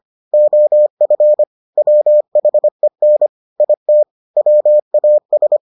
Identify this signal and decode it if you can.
Morse code, heard as a keyed tone, about 25 wpm, 600 Hz. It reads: OF WHEN IT WAS